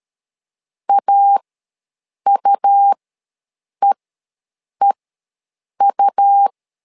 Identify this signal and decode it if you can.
Morse code, heard as a keyed tone, about 13 wpm, spaced as for 7 wpm, 800 Hz. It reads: AUEEU
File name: LEZIONI / 2 UHDN / 2 U ws 12wpm 1655